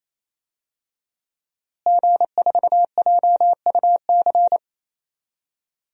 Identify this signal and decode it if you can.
Morse code, heard as a keyed tone, about 27 wpm, 700 Hz. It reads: G4JUC
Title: DXCC-100 Call Signs - 5 Characters - Repeated 3x - 28wpm